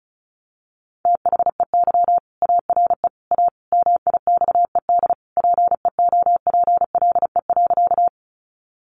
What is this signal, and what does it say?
Morse code, heard as a keyed tone, about 34 wpm, 700 Hz. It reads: THEY ARE A MIXED PEOPLE.
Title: Sentences from Top 1500 Words 35wpm